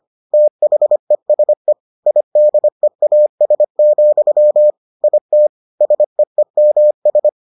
Morse code, heard as a keyed tone, about 25 wpm, 600 Hz. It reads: THESE IDEAS, IT SEEMS